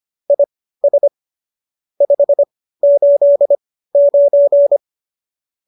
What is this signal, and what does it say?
Morse code, heard as a keyed tone, about 25 wpm, spaced as for 15 wpm, 570 Hz. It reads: IS 589